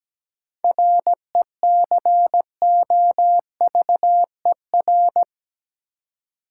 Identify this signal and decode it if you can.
Morse code, heard as a keyed tone, about 17 wpm, 700 Hz. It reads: RECOVER
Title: Seven Letter Words - 17wpm